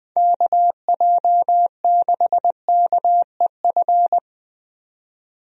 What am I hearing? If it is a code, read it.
Morse code, heard as a keyed tone, about 20 wpm, 700 Hz. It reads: KJ6KEF
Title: USA Call Signs - 6 Characters - 20wpm